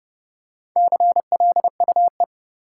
Morse code, heard as a keyed tone, about 30 wpm, 700 Hz. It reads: CLUE
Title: Four Letter Words - 30wpm